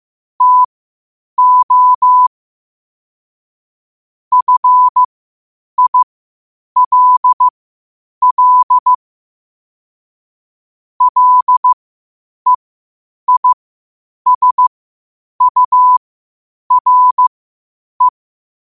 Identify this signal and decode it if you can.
Morse code, heard as a keyed tone, about 15 wpm, spaced as for 8 wpm, 1000 Hz. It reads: TO FILL LEISURE
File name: qotdfiles / 8wpm